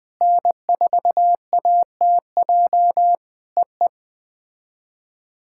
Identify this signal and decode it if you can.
Morse code, heard as a keyed tone, about 20 wpm, 700 Hz. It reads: N4ATJ EE
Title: POTA QSO Practice - 20wpm